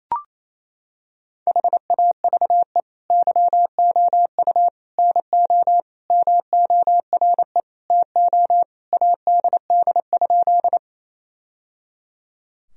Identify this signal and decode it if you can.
Morse code, heard as a keyed tone, about 28 wpm, 700 Hz. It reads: HAVE YOU NO MORE TO ADD?
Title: Sentences from Top 1000 Words - Review Part 1 - 28wpm